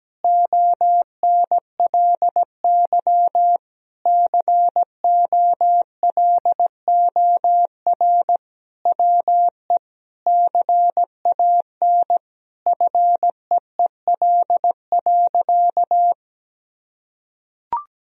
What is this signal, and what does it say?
Morse code, heard as a keyed tone, about 17 wpm, 700 Hz. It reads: ONLY COLOR WE CAN FEEL.
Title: Sentences from Top 1000 Words - Review Part 3 - 17wpm